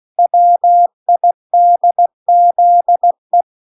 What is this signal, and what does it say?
Morse code, heard as a keyed tone, about 16 wpm, 700 Hz. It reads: WIDZE